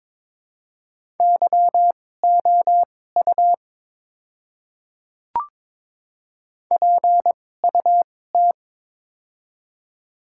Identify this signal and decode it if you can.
Morse code, heard as a keyed tone, about 22 wpm, spaced as for 16 wpm, 700 Hz. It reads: YOU PUT
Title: 2x Character Spacing - Top 200 Words - 22wpm